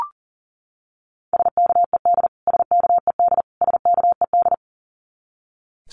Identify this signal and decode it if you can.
Morse code, heard as a keyed tone, about 40 wpm, 700 Hz. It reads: SKED SKED SKED
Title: Abbreviations Repeated 3x - 40wpm